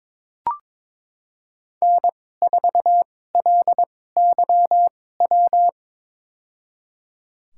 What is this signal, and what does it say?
Morse code, heard as a keyed tone, about 22 wpm, spaced as for 16 wpm, 700 Hz. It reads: N4LYW
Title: USA Call Signs - 5 Characters - 2x Character Spacing - 22wpm